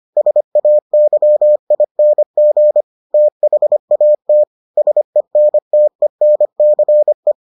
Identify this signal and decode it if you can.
Morse code, heard as a keyed tone, about 25 wpm, 600 Hz. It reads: SAYING THAT SENTENCE